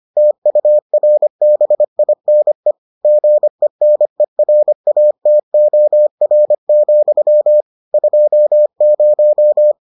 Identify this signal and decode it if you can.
Morse code, heard as a keyed tone, about 25 wpm, 600 Hz. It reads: TURBINE GENERATOR, 20